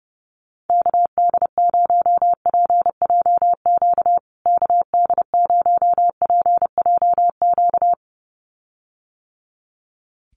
Morse code, heard as a keyed tone, about 30 wpm, 700 Hz. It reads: KD0PJQ KD0PJQ
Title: USA Call Signs Repeated 2x 30wpm